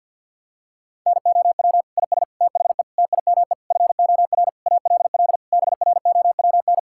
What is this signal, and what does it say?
Morse code, heard as a keyed tone, about 50 wpm, 700 Hz. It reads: NOW IS THE TIME FOR ALL BROWN
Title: Pangram Challenge v2 - 50wpm